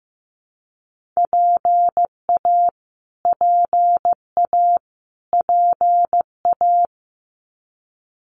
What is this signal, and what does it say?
Morse code, heard as a keyed tone, about 15 wpm, 700 Hz. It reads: PA PA PA
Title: US States Repeated 3x 15wpm